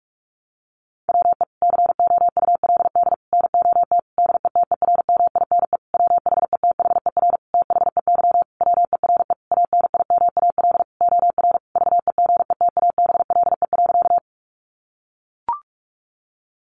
Morse code, heard as a keyed tone, about 44 wpm, 700 Hz. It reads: WE COULD NOT DETERMINE WHETHER THEY WERE ANIMAL OR VEGETABLE.
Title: Sentences from Top 1300 Words 45wpm